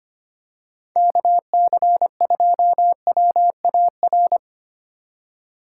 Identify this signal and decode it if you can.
Morse code, heard as a keyed tone, about 25 wpm, 700 Hz. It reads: KC2WAR
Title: USA Call Signs - 6 Characters - 2x Character Spacing - 25wpm